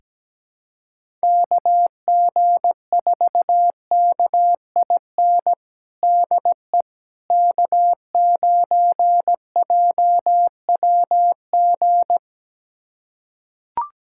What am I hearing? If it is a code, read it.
Morse code, heard as a keyed tone, about 17 wpm, 700 Hz. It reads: KG4KIN DE K9JWG
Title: USA Callsign DE Callsign 17wpm